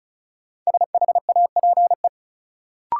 Morse code, heard as a keyed tone, about 36 wpm, 700 Hz. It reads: SHAPE
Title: Sets of 5 Words Encore - New Word Drill - 35wpm